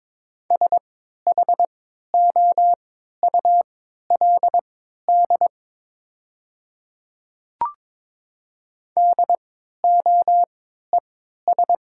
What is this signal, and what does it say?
Morse code, heard as a keyed tone, about 22 wpm, spaced as for 13 wpm, 700 Hz. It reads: SHOULD DOES
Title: 3x Character Spacing - Top 300 Words - 22wpm